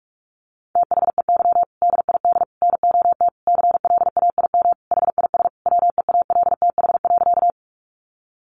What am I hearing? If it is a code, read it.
Morse code, heard as a keyed tone, about 44 wpm, 700 Hz. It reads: THEY DID NOT CLAIM HIS WEALTH.
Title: Sentences from Top 1500 Words 45wpm